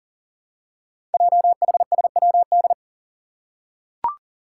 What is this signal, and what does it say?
Morse code, heard as a keyed tone, about 43 wpm, 700 Hz. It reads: JHSWD
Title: Code Groups - 5 Letters - 40wpm